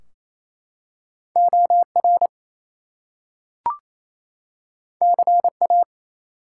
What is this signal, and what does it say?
Morse code, heard as a keyed tone, about 27 wpm, 700 Hz. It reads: OR CA
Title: US States and Canadian Provinces 28wpm